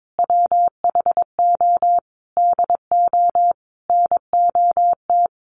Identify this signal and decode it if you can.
Morse code, heard as a keyed tone, about 22 wpm, 700 Hz. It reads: WHO DO NOT
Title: Sentences from Top 1500 Words 22wpm